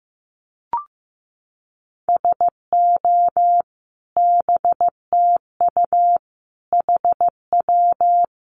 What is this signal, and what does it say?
Morse code, heard as a keyed tone, about 15 wpm, 700 Hz. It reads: SO BTU HW
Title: Common QSO Phrases 15wpm